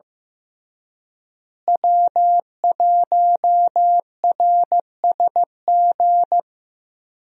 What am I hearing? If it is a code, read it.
Morse code, heard as a keyed tone, about 15 wpm, 700 Hz. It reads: W1RSG
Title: USA Call Signs - 5 Characters - Repeated 3x - 15wpm